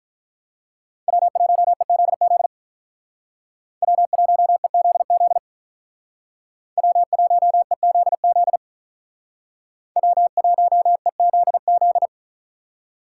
Morse code, heard as a keyed tone, about 45 wpm, 700 Hz. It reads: W1EZZ W1EZZ W1EZZ W1EZZ